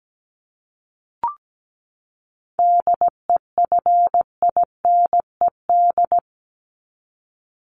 Morse code, heard as a keyed tone, about 17 wpm, 700 Hz. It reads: DEFINED
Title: ED Suffix - Top 100 Words with Base Words 17wpm